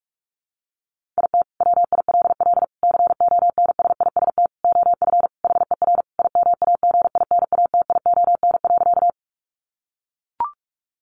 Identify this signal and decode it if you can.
Morse code, heard as a keyed tone, about 44 wpm, 700 Hz. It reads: IT WILL CONSIST OF HER IMAGINATION.
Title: Sentences from Top 1400 Words 45wpm